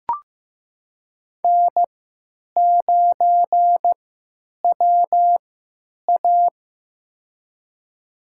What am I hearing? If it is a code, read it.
Morse code, heard as a keyed tone, about 15 wpm, spaced as for 9 wpm, 700 Hz. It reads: N9WA
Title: USA Call Signs - 4 Characters - 3x Character Spacing - 15wpm